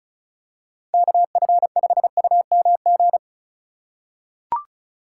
Morse code, heard as a keyed tone, about 36 wpm, 700 Hz. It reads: KF5UMG
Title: USA Call Signs - 6 Characters - 4x Character Spacing - 35wpm